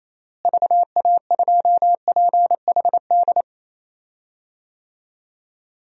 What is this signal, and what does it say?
Morse code, heard as a keyed tone, about 28 wpm, 700 Hz. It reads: VA2PHD
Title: DXCC-100 Call Signs - 6 Characters - Repeated 2x - 28wpm